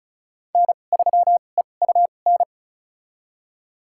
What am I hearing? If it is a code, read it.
Morse code, heard as a keyed tone, about 34 wpm, spaced as for 25 wpm, 700 Hz. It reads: N3EUN
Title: USA Call Signs - 2x Character Spacing - 35wpm